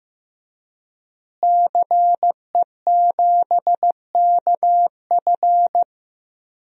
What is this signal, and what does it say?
Morse code, heard as a keyed tone, about 15 wpm, 700 Hz. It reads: CE7KF
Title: DXCC-100 Call Signs - 5 Characters - Repeated 3x - 15wpm